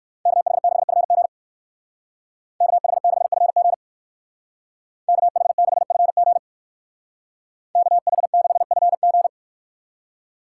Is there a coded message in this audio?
Yes: Morse code from 0.3 to 9.3 s, heard as a keyed tone, about 59 wpm, 700 Hz. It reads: KH6FG KH6FG KH6FG KH6FG